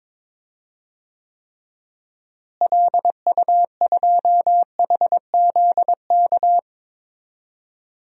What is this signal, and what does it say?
Morse code, heard as a keyed tone, about 22 wpm, 700 Hz. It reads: LU2HZK
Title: DXCC-100 Call Signs - 6 Characters - Repeated 3x - 22wpm